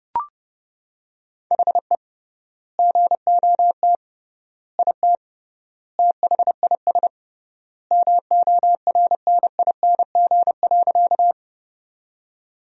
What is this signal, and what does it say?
Morse code, heard as a keyed tone, about 30 wpm, 700 Hz. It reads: HE GOT IT THIS MORNING.
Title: 3x Word-Spacing - Sentences from Top 400 Words - 30wpm